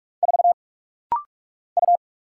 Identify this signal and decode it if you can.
Morse code, heard as a keyed tone, about 44 wpm, 700 Hz. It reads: 4 U
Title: Single Letter-Number Alt - Rapid-Fire 45wpm